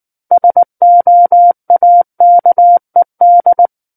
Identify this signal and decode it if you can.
Morse code, heard as a keyed tone, about 19 wpm, 700 Hz. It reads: SOAKED
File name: Book One / 10-Chapter 10 - 19 WPM